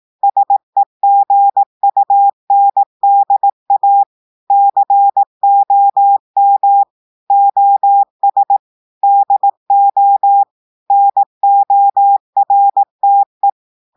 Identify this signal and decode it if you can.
Morse code, heard as a keyed tone, about 18 wpm, 800 Hz. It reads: SEGUNDA COM OS DO NORTE
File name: audios / portugais / portugais texte en clair 1 vitesse 18 mots minutes